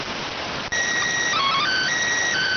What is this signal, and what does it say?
SSTV (slow-scan television), PD 290.